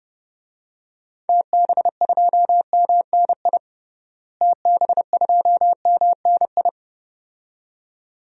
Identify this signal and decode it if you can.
Morse code, heard as a keyed tone, about 32 wpm, 700 Hz. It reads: TB2MNI TB2MNI
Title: DXCC-100 Call Signs - 6 Characters - Repeated 2x - 30wpm